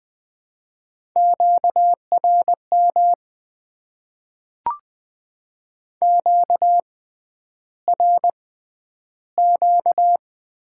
Morse code, heard as a keyed tone, about 20 wpm, spaced as for 10 wpm, 700 Hz. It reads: QRM QRQ
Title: Top 90 QSO Elements - 6x Character Spacing - 20wpm